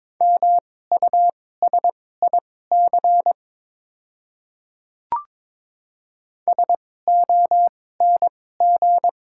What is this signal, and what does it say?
Morse code, heard as a keyed tone, about 22 wpm, spaced as for 16 wpm, 690 Hz. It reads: MUSIC SONG
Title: 2x Character Spacing - Top 500 Words - 22wpm